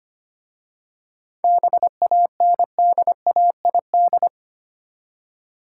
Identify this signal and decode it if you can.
Morse code, heard as a keyed tone, about 25 wpm, 700 Hz. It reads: BANDAID
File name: practice / 25wpm / Easy Words with Hints - 25wpm